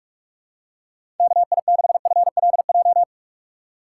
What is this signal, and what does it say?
Morse code, heard as a keyed tone, about 45 wpm, 700 Hz. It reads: KI6FLJ
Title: USA Call Signs - 6 Characters - 4x Character Spacing - 45wpm